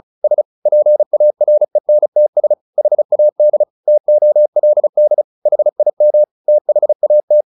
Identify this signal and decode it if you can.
Morse code, heard as a keyed tone, about 34 wpm, 600 Hz. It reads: S PARENTS HAD TOLD HIM THAT
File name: the old man and the sea / the old man and the sea 1 35wpm